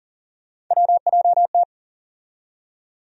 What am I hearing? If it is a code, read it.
Morse code, heard as a keyed tone, about 40 wpm, 700 Hz. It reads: WJT
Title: USA Call Sign Postfixes - 3 Letters 40wpm